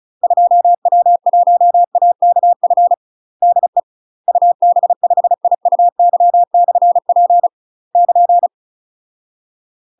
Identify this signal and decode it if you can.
Morse code, heard as a keyed tone, about 34 wpm, 700 Hz. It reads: 2W1AKF DE UB5IUY/P <KN>
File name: UK MASTER QSOs TRAINING / 35WPM / UK MASTER QSOs 0042